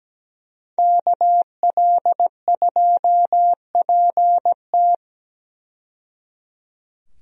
Morse code, heard as a keyed tone, about 17 wpm, 700 Hz. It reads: KL2PT
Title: USA Call Signs - 5 Characters - 17wpm